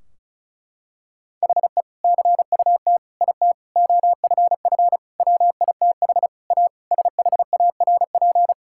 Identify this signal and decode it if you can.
Morse code, heard as a keyed tone, about 36 wpm, 700 Hz. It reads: HE CUT IT OFF WITH A SHARP
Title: Sentences from Top 1600 Words 35wpm